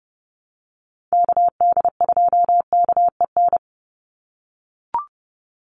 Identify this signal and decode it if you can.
Morse code, heard as a keyed tone, about 30 wpm, 700 Hz. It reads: KD2KEN
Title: USA Call Signs 30wpm